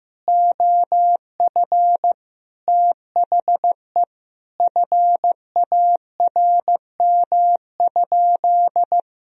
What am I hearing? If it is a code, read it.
Morse code, heard as a keyed tone, about 14 wpm, 700 Hz. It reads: OF THE FARM?
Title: Sentences from Top 1000 Words - Review Part 3 - 15wpm